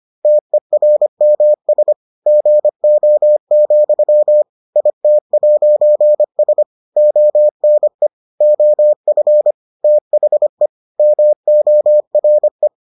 Morse code, heard as a keyed tone, about 25 wpm, 600 Hz. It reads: TERMS GO, IT'S ONE OF THE MORE